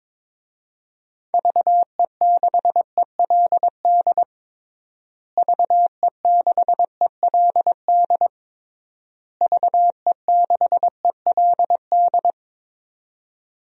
Morse code, heard as a keyed tone, about 22 wpm, 700 Hz. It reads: VE6ELD VE6ELD VE6ELD
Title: DXCC-100 Call Signs - 6 Characters - Repeated 3x - 22wpm